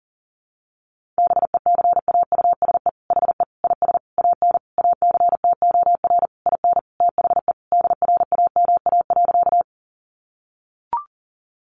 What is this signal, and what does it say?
Morse code, heard as a keyed tone, about 40 wpm, 700 Hz. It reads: BECAUSE HE IS AN ACTOR IN THE DRAMA.
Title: Sentences from Top 1500 Words 40wpm